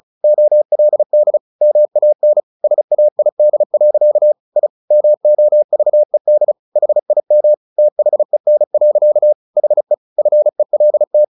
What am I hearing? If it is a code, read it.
Morse code, heard as a keyed tone, about 34 wpm, 600 Hz. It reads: OLD MAN SAID. I MOVED HIM THEN. HE FELT
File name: the old man and the sea 2 35wpm